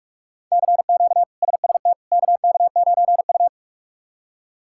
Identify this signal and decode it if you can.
Morse code, heard as a keyed tone, about 46 wpm, 700 Hz. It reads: CQ SST KK9U
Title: K1USN SST QSO Practice - 45wpm